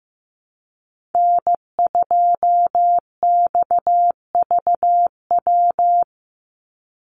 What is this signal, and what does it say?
Morse code, heard as a keyed tone, about 15 wpm, 700 Hz. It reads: N2XVW
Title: USA Call Signs 15wpm